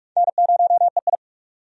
Morse code, heard as a keyed tone, about 46 wpm, 700 Hz. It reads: N0EI
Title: USA Call Signs - 4 Characters - 45wpm